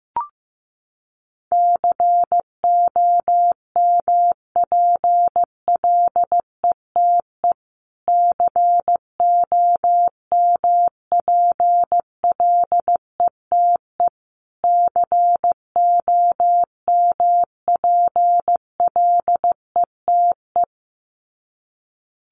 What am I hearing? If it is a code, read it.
Morse code, heard as a keyed tone, about 15 wpm, 700 Hz. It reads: COMPLETE COMPLETE COMPLETE
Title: Top 500 Words Repeated 15wpm